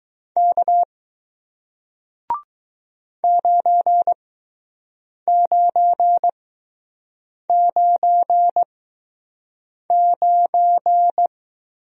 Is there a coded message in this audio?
Yes: Morse code from 0.4 to 11.3 s, heard as a keyed tone, about 20 wpm, spaced as for 6 wpm, 700 Hz. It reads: K 9999